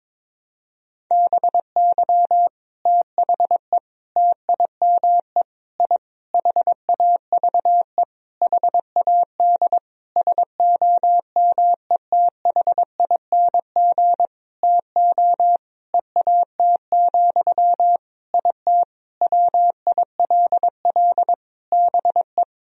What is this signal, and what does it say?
Morse code, heard as a keyed tone, about 23 wpm, 700 Hz. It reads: BY THE TIME I HAVE HAD SOMETHING TO EAT, IT WILL BE
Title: Sentences from Top 1000 Words - Review Part 1 - 22wpm